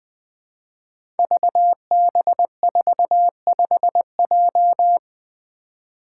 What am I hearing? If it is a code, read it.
Morse code, heard as a keyed tone, about 20 wpm, 700 Hz. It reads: VB45J
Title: Code Groups - 5 Letter:Numbers - 20wpm